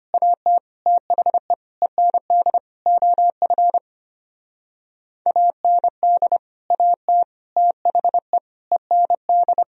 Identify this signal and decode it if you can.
Morse code, heard as a keyed tone, about 32 wpm, 700 Hz. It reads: AT THE END OF AND AT THE END